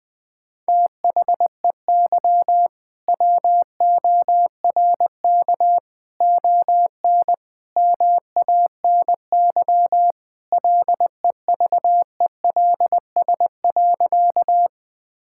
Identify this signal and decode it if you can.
Morse code, heard as a keyed tone, about 20 wpm, 700 Hz. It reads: THEY WORK ON MANY LEVELS.
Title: All-Age Appropriate Jokes - 20wpm